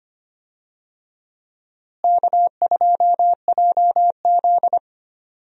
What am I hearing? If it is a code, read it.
Morse code, heard as a keyed tone, about 25 wpm, 700 Hz. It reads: K2JZ
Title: USA Call Signs - 4 Characters - Repeated 3x - 25wpm